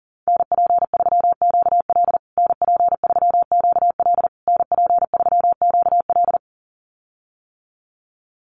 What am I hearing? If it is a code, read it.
Morse code, heard as a keyed tone, about 40 wpm, 700 Hz. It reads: NP3QL NP3QL NP3QL
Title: USA Call Signs Repeated 3x 40wpm